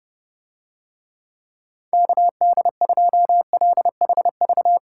Morse code, heard as a keyed tone, about 32 wpm, 700 Hz. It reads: KD2LHV